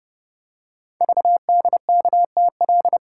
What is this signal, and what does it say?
Morse code, heard as a keyed tone, about 29 wpm, 700 Hz. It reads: VDKTL
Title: Code Groups - 5 Letters - 30wpm